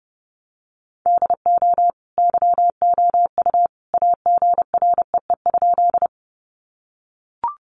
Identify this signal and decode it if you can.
Morse code, heard as a keyed tone, about 30 wpm, 700 Hz. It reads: DO YOU AGREE?
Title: Sentences from Top 1100 Words 30wpm